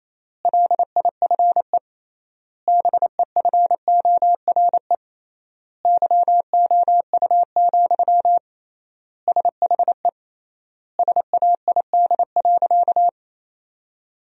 Morse code, heard as a keyed tone, about 28 wpm, 700 Hz. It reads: LIFE BEFORE YOU, SHE SAID.
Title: 3x Word-Spacing - Sentences from Top 400 Words - 28wpm